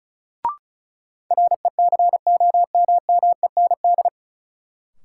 Morse code, heard as a keyed tone, about 36 wpm, 700 Hz. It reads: RECOMMEND
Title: Sentences from Top 1200 Words - New Word Drill - 35wpm